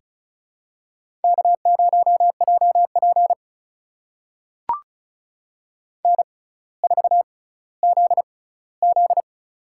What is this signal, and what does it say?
Morse code, heard as a keyed tone, about 36 wpm, spaced as for 18 wpm, 700 Hz. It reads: K0JP N4ZZ
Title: USA Call Signs - 4 Characters - 6x Character Spacing - 35wpm